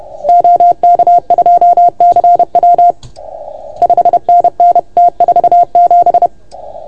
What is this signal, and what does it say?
Morse code, heard as a keyed tone, about 30 wpm, 690 Hz. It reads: OK2CW 5NN T47
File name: Archiv / CUC381 / OK1FGD CUC 381